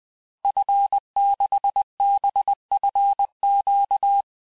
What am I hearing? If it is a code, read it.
Morse code, heard as a keyed tone, about 20 wpm, 800 Hz. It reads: F6BFQ
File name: cours-du-lundi files / cours-1584-Fin-de-QSO-vitesse-19mpm